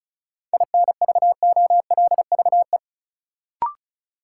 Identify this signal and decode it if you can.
Morse code, heard as a keyed tone, about 36 wpm, 700 Hz. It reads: INVOLVE
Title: Sentences from Top 1200 Words - New Word Drill - 35wpm